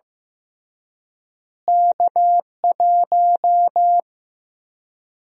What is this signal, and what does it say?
Morse code, heard as a keyed tone, about 15 wpm, 700 Hz. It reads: K1
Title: USA Call Sign Prefixes - Repeated 2x - 15wpm